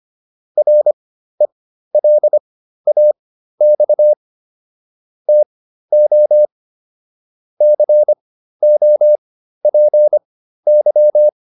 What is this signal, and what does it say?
Morse code, heard as a keyed tone, about 25 wpm, spaced as for 13 wpm, 600 Hz. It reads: RELAX TO COPY
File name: PR404 13